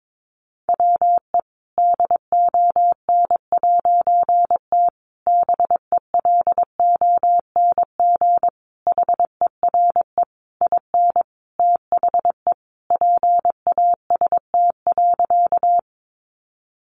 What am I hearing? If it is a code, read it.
Morse code, heard as a keyed tone, about 22 wpm, 700 Hz. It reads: WE DON'T BELONG HERE IN THE PAST.
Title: A Sound Of Thunder 22wpm